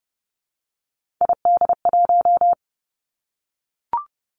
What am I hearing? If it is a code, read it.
Morse code, heard as a keyed tone, about 30 wpm, 700 Hz. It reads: ID1
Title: Three Challenging Letter-Number Combinations 30wpm